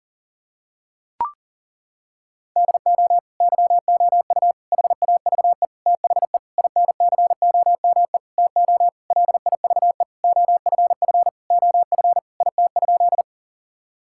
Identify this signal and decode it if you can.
Morse code, heard as a keyed tone, about 40 wpm, 700 Hz. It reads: DO YOU HAVE THE INCOME TO LIVE OFF OF IT?
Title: Sentences from Top 1000 Words - Review Part 5 - 40wpm